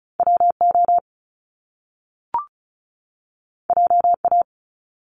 Morse code, heard as a keyed tone, about 36 wpm, 700 Hz. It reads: WO JA
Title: Top Two Letter Combinations in Words 35wpm